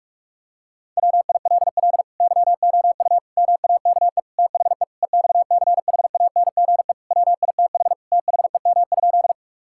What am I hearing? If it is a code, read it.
Morse code, heard as a keyed tone, about 46 wpm, 700 Hz. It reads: WILL YOU MAKE THE EXCHANGE WITH THEM?
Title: Sentences from Top 1000 Words - Review Part 6 - 45wpm